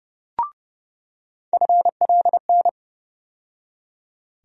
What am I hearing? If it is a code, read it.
Morse code, heard as a keyed tone, about 30 wpm, 700 Hz. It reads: FLN
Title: USA Call Sign Postfixes - 3 Letters 30wpm